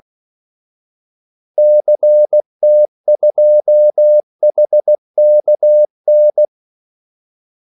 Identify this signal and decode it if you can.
Morse code, heard as a keyed tone, about 16 wpm, 600 Hz. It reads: CT2HKN
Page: Kaldesignaler 16 wpm